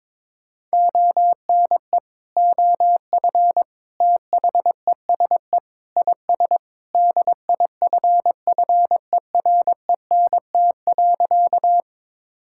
Morse code, heard as a keyed tone, about 22 wpm, 700 Hz. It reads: ONE OF THESE IS DIFFERENT.
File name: practice / 22wpm / Sentences with 2-Word Phrases - 22wpm